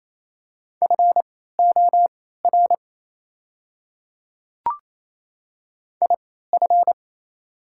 Morse code, heard as a keyed tone, about 28 wpm, spaced as for 16 wpm, 700 Hz. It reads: FOR IF
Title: 3x Character Spacing - Top 100 Words - 28wpm